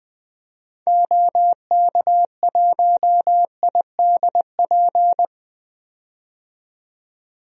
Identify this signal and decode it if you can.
Morse code, heard as a keyed tone, about 20 wpm, 700 Hz. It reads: OK1IDP
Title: DXCC-100 Call Signs - 6 Characters - 20wpm